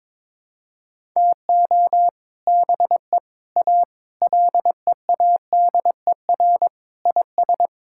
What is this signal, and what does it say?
Morse code, heard as a keyed tone, about 22 wpm, 700 Hz. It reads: TO BE A LEADER IS
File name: practice / 22wpm / Sentences with 3-Word Phrases - 22wpm